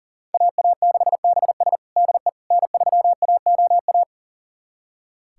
Morse code, heard as a keyed tone, about 40 wpm, 700 Hz. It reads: AA6BS DE N3AOA
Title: USA Callsign DE Callsign - 5 Characters - 40wpm